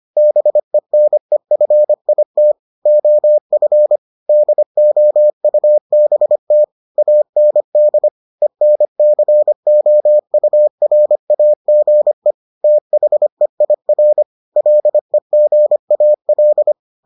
Morse code, heard as a keyed tone, about 25 wpm, 600 Hz. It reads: BENEFIT OF DOUBT AND ENCOURAGE THEIR LEGAL